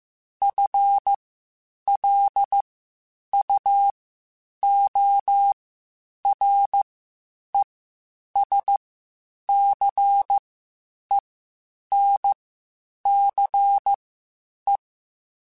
Morse code, heard as a keyed tone, about 15 wpm, spaced as for 8 wpm, 800 Hz. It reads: FLUORESCENCE